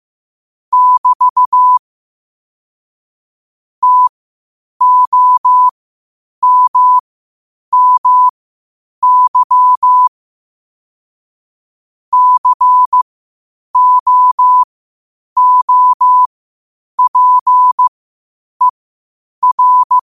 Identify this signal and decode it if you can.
Morse code, heard as a keyed tone, about 15 wpm, spaced as for 8 wpm, 1000 Hz. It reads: <BT> TOMMY COOPER
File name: qotdfiles / 8wpm